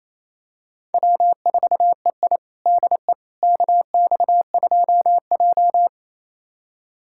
Morse code, heard as a keyed tone, about 27 wpm, 700 Hz. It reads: W4EI DE KX2J
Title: USA Callsign DE Callsign - 4 Characters - 28wpm